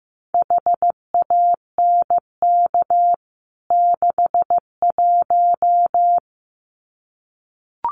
Morse code, heard as a keyed tone, about 15 wpm, 700 Hz. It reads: HANK 61
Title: CWT Exchange 15wpm